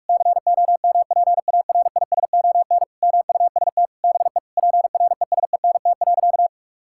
Morse code, heard as a keyed tone, about 46 wpm, 700 Hz. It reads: COMPARISON MUST BE PRESENT.
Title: Sentences from Top 1600 Words 45wpm